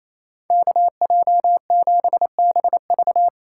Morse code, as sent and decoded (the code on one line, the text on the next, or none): -.- .--- --... -... ...-
KJ7BV